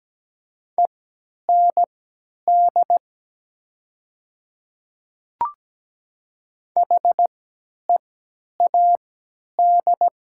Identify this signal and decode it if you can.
Morse code, heard as a keyed tone, about 17 wpm, spaced as for 10 wpm, 700 Hz. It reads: END HEAD
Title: 3x Character Spacing - Top 500 Words - 17wpm